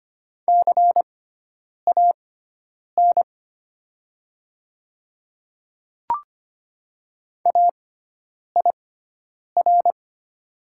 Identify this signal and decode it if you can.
Morse code, heard as a keyed tone, about 25 wpm, spaced as for 9 wpm, 700 Hz. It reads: CAN AIR
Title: 6x Character Spacing - Top 200 Words - 25wpm